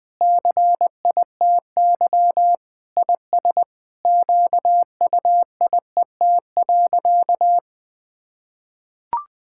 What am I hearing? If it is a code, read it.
Morse code, heard as a keyed tone, about 20 wpm, 700 Hz. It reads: CITY IS QUIET.
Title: Sentences with 7-Word Phrases - 20wpm